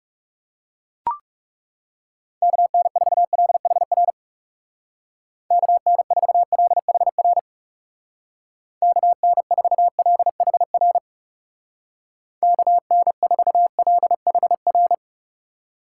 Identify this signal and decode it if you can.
Morse code, heard as a keyed tone, about 37 wpm, 700 Hz. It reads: KN4LHR KN4LHR KN4LHR KN4LHR
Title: Speed Racing - USA Call Signs 30wpm